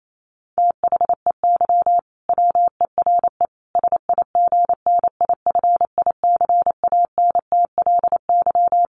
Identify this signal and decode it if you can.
Morse code, heard as a keyed tone, about 28 wpm, 700 Hz. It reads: THEY WERE SIGNIFICANTLY
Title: Sentences from Top 1500 Words 28wpm